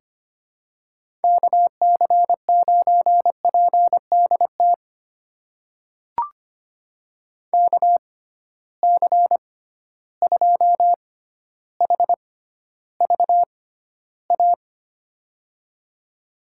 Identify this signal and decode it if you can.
Morse code, heard as a keyed tone, about 25 wpm, spaced as for 13 wpm, 700 Hz. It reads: KC9PDT KC2HVA
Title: USA Call Signs - 6 Characters - 6x Character Spacing - 25wpm